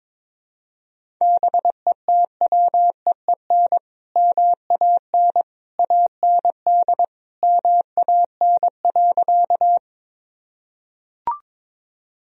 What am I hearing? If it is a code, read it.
Morse code, heard as a keyed tone, about 22 wpm, 700 Hz. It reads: BETWEEN MAN AND MAN.
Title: Sentences from Top 1000 Words - Review Part 1 - 22wpm